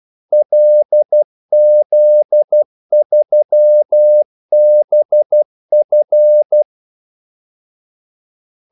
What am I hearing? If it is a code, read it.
Morse code, heard as a keyed tone, about 12 wpm, 600 Hz. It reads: LZ3BF
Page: 2 ekstra ordmellemrum mellem kaldesignalerne